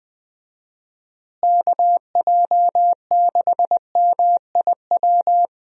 Morse code, heard as a keyed tone, about 20 wpm, 700 Hz. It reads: KJ6MIW
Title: USA Call Signs - 6 Characters - Repeated 2x - 20wpm